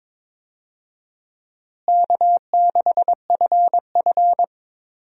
Morse code, heard as a keyed tone, about 22 wpm, 700 Hz. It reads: K6FF
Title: USA Call Signs - 4 Characters - Repeated 2x - 22wpm